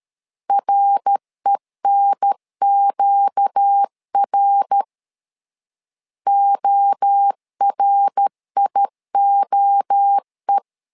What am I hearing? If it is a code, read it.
Morse code, heard as a keyed tone, about 13 wpm, 800 Hz. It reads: RENQR ORIOE
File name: LEZIONI / 4 Q1 12wpm 2028